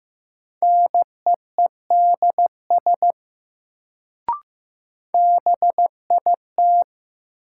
Morse code, heard as a keyed tone, about 15 wpm, 700 Hz. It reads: NEEDS BIT
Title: Sets of 3 Words - New Word Drill - 15wpm